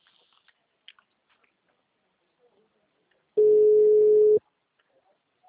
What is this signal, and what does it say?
Telephone call progress tone (Europe, 425 Hz), heard as a telephone network's tone, about 420 Hz. Ringback tone.